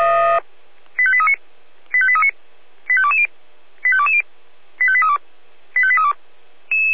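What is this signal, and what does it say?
Five-tone selective call (ZVEI), heard as quick runs of five tones, tones of about 70 ms. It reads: 85839 85839 85220 85220 85832 85832
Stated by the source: ZVEI five-tone call 85267 received by SDR.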